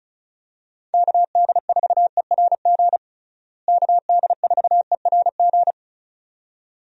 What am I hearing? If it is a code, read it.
Morse code, heard as a keyed tone, about 34 wpm, 700 Hz. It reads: KD4ERG KD4ERG
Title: USA Call Signs - 6 Characters - Repeated 2x - 35wpm